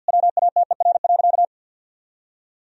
Morse code, heard as a keyed tone, about 51 wpm, 700 Hz. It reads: WATER.
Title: Sentences from Top 1000 Words - Review Part 3 - 50wpm